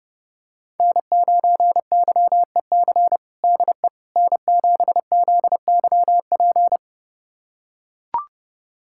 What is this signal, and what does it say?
Morse code, heard as a keyed tone, about 32 wpm, 700 Hz. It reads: N9YEC DE N7ZYP
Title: USA Callsign DE Callsign - 5 Characters - 30wpm